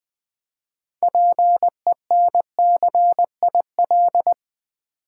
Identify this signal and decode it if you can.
Morse code, heard as a keyed tone, about 20 wpm, 700 Hz. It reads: PENCIL
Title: Build A Word - 2 Letters At A Time - 6-Letter Words - 20wpm